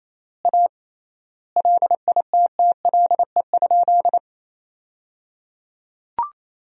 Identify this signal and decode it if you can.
Morse code, heard as a keyed tone, about 27 wpm, 700 Hz. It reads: A LITTLE?
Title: 3x Word-Spacing - Sentences from Top 500 Words - 28wpm